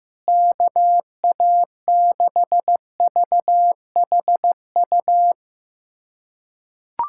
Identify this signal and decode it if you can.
Morse code, heard as a keyed tone, about 15 wpm, 700 Hz. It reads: KA6VHU
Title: USA Callsign DE Callsign 15wpm